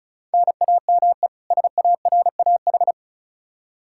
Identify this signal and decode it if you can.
Morse code, heard as a keyed tone, about 34 wpm, 700 Hz. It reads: NAME SARAH
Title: K1USN SST QSO Practice 35wpm